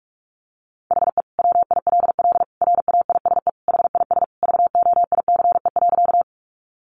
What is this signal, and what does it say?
Morse code, heard as a keyed tone, about 44 wpm, 700 Hz. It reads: HE WILL RAISE HIS VOICE.
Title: Sentences from Top 1400 Words 45wpm